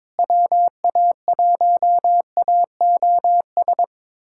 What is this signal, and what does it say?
Morse code, heard as a keyed tone, about 22 wpm, 700 Hz. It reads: WA1AOS
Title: USA Callsign DE Callsign - 6 Characters - 22wpm